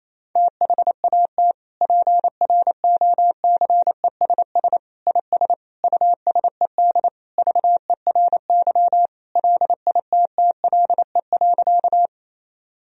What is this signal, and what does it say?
Morse code, heard as a keyed tone, about 28 wpm, 700 Hz. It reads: THAT PROCESS IS USED VERY LITTLE.
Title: Sentences from Top 1000 Words - Review Part 3 - 28wpm